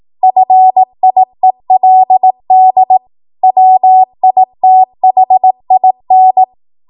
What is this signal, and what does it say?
Morse code, heard as a keyed tone, about 18 wpm, 750 Hz. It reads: FIELD WITHIN
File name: Morse / 18WPM